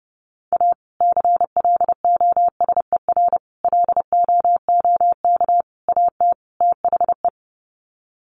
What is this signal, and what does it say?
Morse code, heard as a keyed tone, about 30 wpm, 700 Hz. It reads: A CLOSER LOOK AT THE
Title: Sets of 5 Words Encore 30wpm